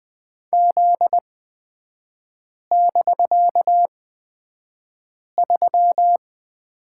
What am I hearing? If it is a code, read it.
Morse code, heard as a keyed tone, about 20 wpm, spaced as for 5 wpm, 700 Hz. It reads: Z<BK>3
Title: 250 Lesson 40e - BK - Three Characters 20wpm